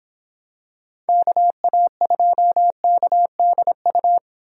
Morse code, heard as a keyed tone, about 26 wpm, 700 Hz. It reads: KA2KDU